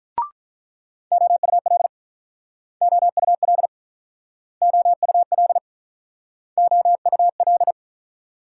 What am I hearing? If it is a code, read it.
Morse code, heard as a keyed tone, about 47 wpm, 700 Hz. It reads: OUL OUL OUL OUL